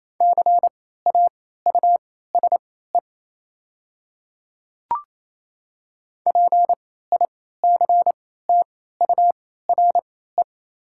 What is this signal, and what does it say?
Morse code, heard as a keyed tone, about 28 wpm, spaced as for 16 wpm, 700 Hz. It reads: CAUSE PICTURE
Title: 3x Character Spacing - Top 300 Words - 28wpm